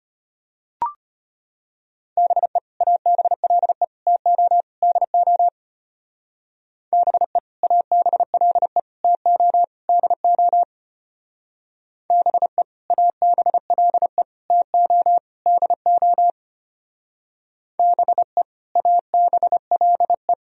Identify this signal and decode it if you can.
Morse code, heard as a keyed tone, about 32 wpm, 700 Hz. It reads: BE ABLE TO DO BE ABLE TO DO BE ABLE TO DO BE ABLE